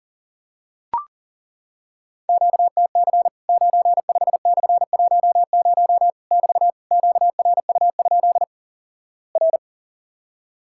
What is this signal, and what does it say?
Morse code, heard as a keyed tone, about 37 wpm, 700 Hz. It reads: QTC 95/10 <BT> QRU? R
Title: Worked All Europe DX Contest - QTC Exchange - 40wpm